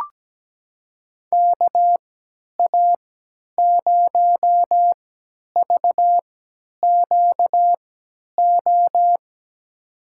Morse code, heard as a keyed tone, about 17 wpm, spaced as for 10 wpm, 700 Hz. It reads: KA0VQO
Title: USA Call Signs - 6 Characters - 3x Character Spacing - 17wpm